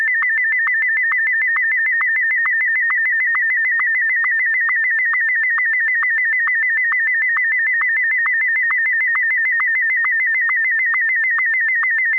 SSTV (slow-scan television), Martin 1.